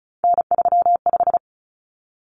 Morse code, heard as a keyed tone, about 34 wpm, 700 Hz. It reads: N35
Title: Common Cut Numbers - 3-Digit 35wpm